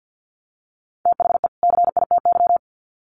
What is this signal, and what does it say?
Morse code, heard as a keyed tone, about 48 wpm, 700 Hz. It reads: THE CITY
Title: Sets of 2 Words 50wpm